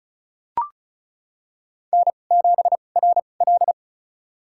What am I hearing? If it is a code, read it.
Morse code, heard as a keyed tone, about 34 wpm, spaced as for 25 wpm, 700 Hz. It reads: N7RL
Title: USA Call Signs - 4 Characters - 2x Character Spacing - 35wpm